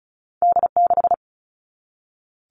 Morse code, heard as a keyed tone, about 36 wpm, 700 Hz. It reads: D6
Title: All Cut Numbers - 2-Digit 35wpm